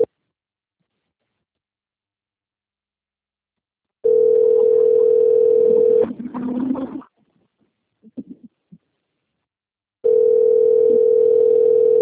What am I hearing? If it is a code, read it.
Ringback tone (North America), 440 + 480 Hz.